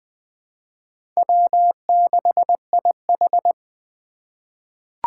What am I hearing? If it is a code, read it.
Morse code, heard as a keyed tone, about 20 wpm, 700 Hz. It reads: W6IH
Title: USA Call Signs - 4 Characters - 2x Character Spacing - 20wpm